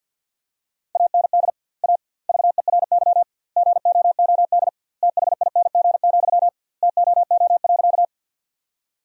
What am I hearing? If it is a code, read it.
Morse code, heard as a keyed tone, about 51 wpm, 700 Hz. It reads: AND A VERY GOOD THING, TOO.
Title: 2x Word-Spacing - Sentences from Top 200 Words - 50wpm